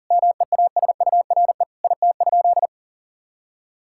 Morse code, heard as a keyed tone, about 40 wpm, 700 Hz. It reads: MEASURE IT?
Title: Sentences from Top 1000 Words - Review Part 6 - 40wpm